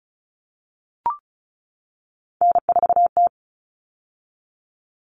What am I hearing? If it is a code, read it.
Morse code, heard as a keyed tone, about 34 wpm, 700 Hz. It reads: N4T
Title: All Cut Numbers - 3-Digit 35wpm